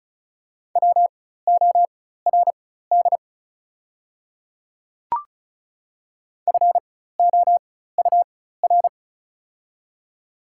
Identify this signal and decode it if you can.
Morse code, heard as a keyed tone, about 36 wpm, spaced as for 16 wpm, 700 Hz. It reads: WORD FOUR
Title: 4x Character Spacing - Top 300 Words - 35wpm